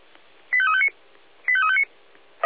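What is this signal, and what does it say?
Five-tone selective call (ZVEI), heard as quick runs of five tones, tones of about 70 ms. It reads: 85359 85359
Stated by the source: ZVEI five-tone selective call received on an SDR.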